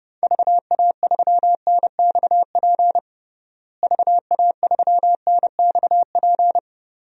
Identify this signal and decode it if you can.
Morse code, heard as a keyed tone, about 32 wpm, 700 Hz. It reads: VA3NXP VA3NXP
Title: DXCC-100 Call Signs - 6 Characters - Repeated 2x - 30wpm